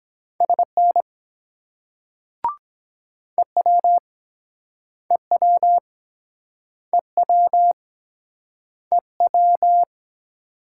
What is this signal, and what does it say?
Morse code, heard as a keyed tone, about 24 wpm, 700 Hz. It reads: SN EW EW EW EW